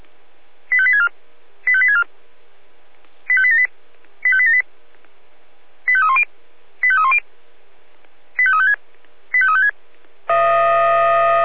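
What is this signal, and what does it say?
Five-tone selective call (ZVEI), heard as quick runs of five tones, tones of about 70 ms. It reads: 85864 85864 85878 85878 85310 85310 85356 85356
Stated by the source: ZVEI five-tone selective call 85889 received on an SDR.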